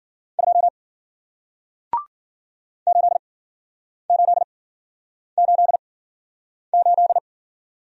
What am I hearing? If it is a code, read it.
Morse code, heard as a keyed tone, about 53 wpm, spaced as for 9 wpm, 700 Hz. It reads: 2 8888